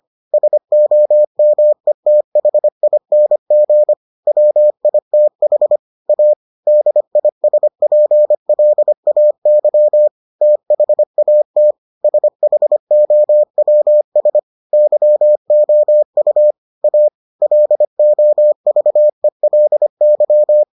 Morse code, heard as a keyed tone, about 25 wpm, 600 Hz. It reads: SOMETHING WITH A DISPLAY THAT SHOWS YOU A LOVELY